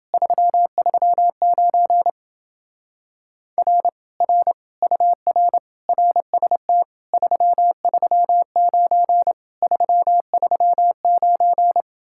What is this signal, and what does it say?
Morse code, heard as a keyed tone, about 28 wpm, 700 Hz. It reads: 339 R R UR RST 339 339